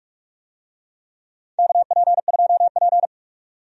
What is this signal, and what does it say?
Morse code, heard as a keyed tone, about 46 wpm, 700 Hz. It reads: KP2P
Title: DXCC-100 Call Signs - 4 Characters - 45wpm